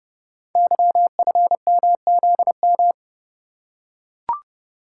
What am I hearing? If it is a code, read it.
Morse code, heard as a keyed tone, about 29 wpm, 700 Hz. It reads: YFMZM
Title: Code Groups - 5 Letters - 30wpm